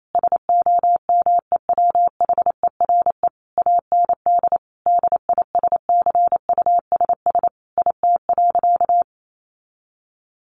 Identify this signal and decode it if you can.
Morse code, heard as a keyed tone, about 28 wpm, 700 Hz. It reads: SOMEWHERE AND DISCUSS IT.
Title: Sentences from Top 1400 Words 28wpm